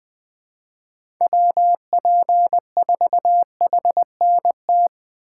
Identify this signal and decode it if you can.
Morse code, heard as a keyed tone, about 20 wpm, 700 Hz. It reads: WP4HNT
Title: USA Call Signs - 3x Character Spacing - 20wpm